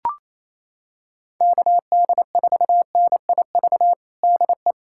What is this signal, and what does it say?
Morse code, heard as a keyed tone, about 28 wpm, 700 Hz. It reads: KD4NIV DE
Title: USA Callsign DE Callsign 28wpm